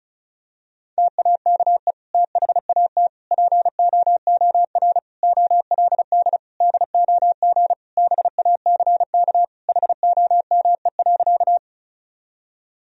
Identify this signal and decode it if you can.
Morse code, heard as a keyed tone, about 36 wpm, 700 Hz. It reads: TAKE THAT POOR OLD DOG BACK HOME.
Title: Sentences from Top 1000 Words - Review Part 2 - 35wpm